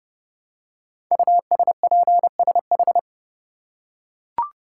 Morse code, heard as a keyed tone, about 30 wpm, 700 Hz. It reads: USPSH
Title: Code Groups - 5 Letters - 30wpm